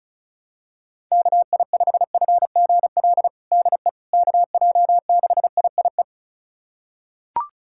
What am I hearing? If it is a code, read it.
Morse code, heard as a keyed tone, about 36 wpm, 700 Hz. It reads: KI5FGL DE KJ6IIE
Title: USA Callsign DE Callsign - 6 Characters - 35wpm